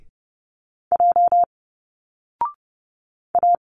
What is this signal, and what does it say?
Morse code, heard as a keyed tone, about 30 wpm, 700 Hz. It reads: J A
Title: Single Letters 30wpm